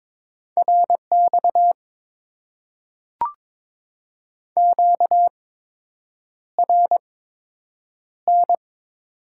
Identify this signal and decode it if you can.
Morse code, heard as a keyed tone, about 22 wpm, spaced as for 8 wpm, 700 Hz. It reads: RX QRN
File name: Top 90 QSO Elements - 8x Character Spacing - 22wpm